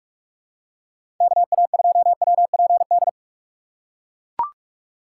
Morse code, heard as a keyed tone, about 44 wpm, 700 Hz. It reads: KA2WPD
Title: USA Call Signs - 6 Characters - 6x Character Spacing - 45wpm